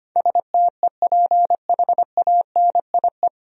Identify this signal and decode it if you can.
Morse code, heard as a keyed tone, about 25 wpm, 700 Hz. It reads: STEPHANIE
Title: USA Common Names - 25wpm